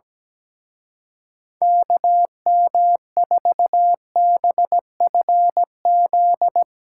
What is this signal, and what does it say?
Morse code, heard as a keyed tone, about 18 wpm, 700 Hz. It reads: KM4BFZ
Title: Speed Racing - USA Call Signs 17wpm